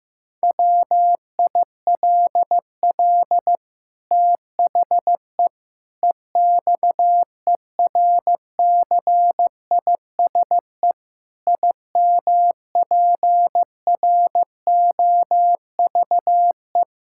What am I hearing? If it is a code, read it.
Morse code, heard as a keyed tone, about 16 wpm, 700 Hz. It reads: WILL THE EXERCISE IMPROVE
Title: Sentences from Top 1000 Words - Review Part 6 - 15wpm